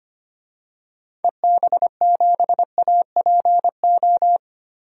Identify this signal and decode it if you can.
Morse code, heard as a keyed tone, about 25 wpm, 700 Hz. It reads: EB7APO
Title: DXCC-100 Call Signs - 6 Characters - 25wpm